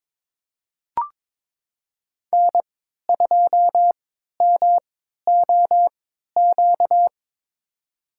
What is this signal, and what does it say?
Morse code, heard as a keyed tone, about 22 wpm, spaced as for 13 wpm, 700 Hz. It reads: N2MOQ
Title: USA Call Signs - 5 Characters - 3x Character Spacing - 22wpm